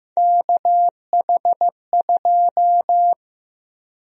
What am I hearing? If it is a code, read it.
Morse code, heard as a keyed tone, about 15 wpm, 700 Hz. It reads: KH2
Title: USA Call Sign Prefixes - 15wpm